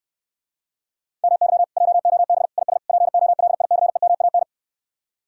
Morse code, heard as a keyed tone, about 69 wpm, 700 Hz. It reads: MY JOB IS POLICEMAN